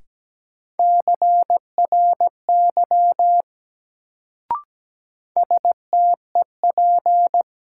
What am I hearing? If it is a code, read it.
Morse code, heard as a keyed tone, about 17 wpm, 700 Hz. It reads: CRY STEP
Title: Sets of 5 Words Encore - New Word Drill - 17wpm